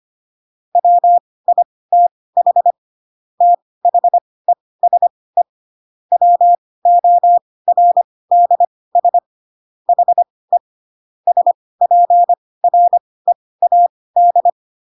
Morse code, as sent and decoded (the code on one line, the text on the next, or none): .-- .. - .... / - .... ... / .-- --- .-. -.. ... / .... / ... .--. .-. .- -..
WITH THESE WORDS HE SPREAD